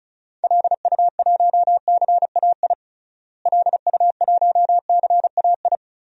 Morse code, heard as a keyed tone, about 36 wpm, 700 Hz. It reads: LU1CAI LU1CAI
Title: DXCC-100 Call Signs - 6 Characters - Repeated 3x - 35wpm